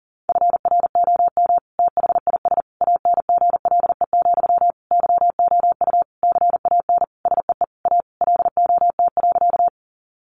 Morse code, heard as a keyed tone, about 37 wpm, 700 Hz. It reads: FROM THIS ANGLE, YOU CAN SEE A LOT.
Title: Sentences from Top 1500 Words 40wpm